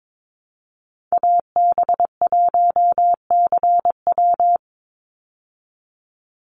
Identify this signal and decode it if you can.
Morse code, heard as a keyed tone, about 22 wpm, 700 Hz. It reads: AB1CW
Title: Call Signs Repeated 3x - 22wpm